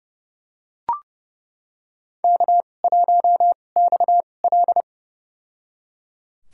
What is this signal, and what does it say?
Morse code, heard as a keyed tone, about 30 wpm, spaced as for 22 wpm, 700 Hz. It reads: K1XL
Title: USA Call Signs - 4 Characters - 2x Character Spacing - 30wpm